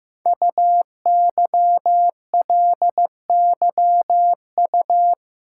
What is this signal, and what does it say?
Morse code, heard as a keyed tone, about 15 wpm, 700 Hz. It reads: UYLYU
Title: Code Groups - 5 Letters - 15wpm